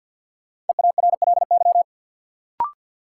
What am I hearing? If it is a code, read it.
Morse code, heard as a keyed tone, about 47 wpm, 700 Hz. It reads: EARLY
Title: Sentences from Top 400 Words - New Word Drill - 50wpm